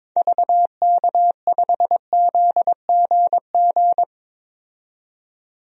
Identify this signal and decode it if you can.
Morse code, heard as a keyed tone, about 22 wpm, 700 Hz. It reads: VK5ZGG
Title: DXCC-100 Call Signs - 6 Characters - 22wpm